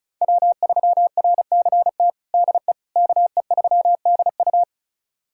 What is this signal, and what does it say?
Morse code, heard as a keyed tone, about 36 wpm, 700 Hz. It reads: W3RCT DE KE3DU
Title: USA Callsign DE Callsign - 5 Characters - 35wpm